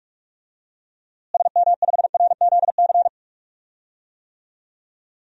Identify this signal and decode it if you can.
Morse code, heard as a keyed tone, about 46 wpm, 700 Hz. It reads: SM5RZC
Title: DXCC-100 Call Signs - 6 Characters - Repeated 2x - 45wpm